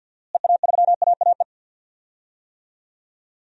Morse code, heard as a keyed tone, about 46 wpm, 700 Hz. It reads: EA3AAE
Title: DXCC-100 Call Signs - 6 Characters - Repeated 3x - 50wpm